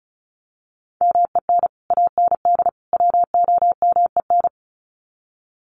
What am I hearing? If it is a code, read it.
Morse code, heard as a keyed tone, about 34 wpm, 700 Hz. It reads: MEN AND WOMEN
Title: Sets of 3 Words 35wpm